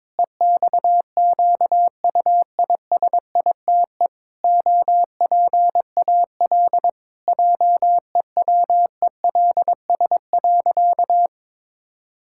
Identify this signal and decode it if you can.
Morse code, heard as a keyed tone, about 22 wpm, 700 Hz. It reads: EXQUISITE OPAL JEWELS.
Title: Pangram Challenge v2 - 22wpm